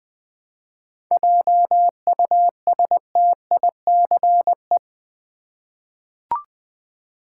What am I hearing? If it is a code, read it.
Morse code, heard as a keyed tone, about 20 wpm, 700 Hz. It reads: JUSTICE
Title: 7 Letter Words - First 3 Letters Given - 20wpm